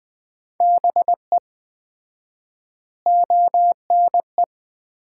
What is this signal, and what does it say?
Morse code, heard as a keyed tone, about 21 wpm, 700 Hz. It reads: BE ONE
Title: 4x Word-Spacing - Sets of 4 Words - 20wpm